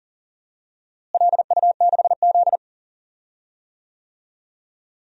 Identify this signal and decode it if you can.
Morse code, heard as a keyed tone, about 40 wpm, 700 Hz. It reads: LU6Z
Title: DXCC-100 Call Signs - 4 Characters - Repeated 2x - 40wpm